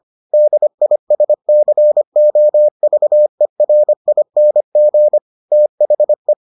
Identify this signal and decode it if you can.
Morse code, heard as a keyed tone, about 25 wpm, 600 Hz. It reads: DISCOVERING THE